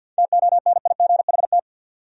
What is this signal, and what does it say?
Morse code, heard as a keyed tone, about 46 wpm, 700 Hz. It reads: TONIGHT
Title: Sentences from Top 600 Words - New Word Drill - 50wpm